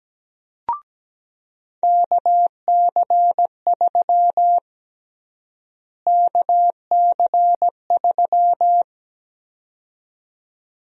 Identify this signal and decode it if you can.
Morse code, heard as a keyed tone, about 17 wpm, 700 Hz. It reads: KC3 KC3
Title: USA Call Sign Prefixes - Repeated 2x - 17wpm